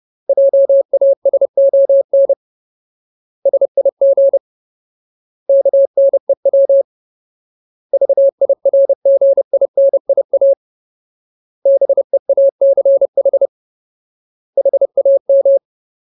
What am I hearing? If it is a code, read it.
Morse code, heard as a keyed tone, about 30 wpm, 550 Hz. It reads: JASON SIG KNEW VIRGINIA BEACH HAM
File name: Ultimate Word List / 30WPM Ultimate